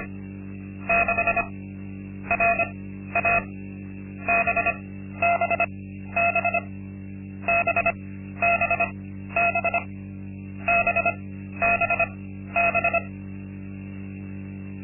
Morse code, heard as a keyed tone, about 25 wpm, spaced as for 13 wpm, 690 Hz. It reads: 6 RA BBB BBB BBB
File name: Linksmakalnis-2